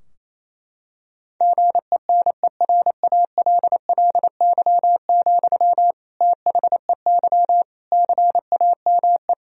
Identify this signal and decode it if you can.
Morse code, heard as a keyed tone, about 28 wpm, 700 Hz. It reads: GENERALLY, THEY CAME
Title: Sentences from Top 1000 Words - Review Part 4 - 28wpm